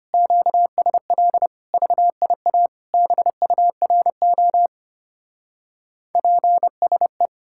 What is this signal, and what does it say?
Morse code, heard as a keyed tone, about 29 wpm, 700 Hz. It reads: QSL VIA BURO PSE
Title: Speed-Racing Common QSO Phrases Encore - 25wpm